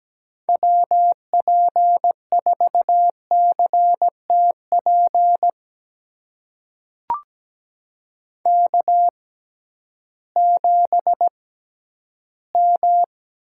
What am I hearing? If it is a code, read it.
Morse code, heard as a keyed tone, about 17 wpm, spaced as for 11 wpm, 700 Hz. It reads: WP4CTP K7M
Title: USA Call Signs - 6x Character Spacing - 17wpm